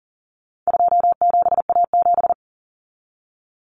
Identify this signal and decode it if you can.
Morse code, heard as a keyed tone, about 40 wpm, 700 Hz. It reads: 27A7
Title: Common Cut Numbers - 4-Digit 40wpm